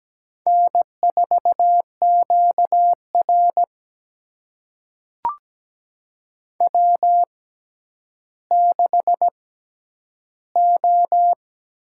Morse code, heard as a keyed tone, about 17 wpm, spaced as for 10 wpm, 700 Hz. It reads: N4QR W6O
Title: USA Call Signs - 4 Characters - 6x Character Spacing - 17wpm